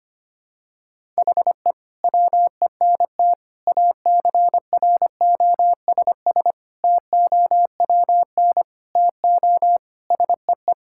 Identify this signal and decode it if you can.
Morse code, heard as a keyed tone, about 25 wpm, 700 Hz. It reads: HE WENT ACROSS TOWN TO SEE
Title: Sentences from Top 1000 Words - Review Part 5 - 25wpm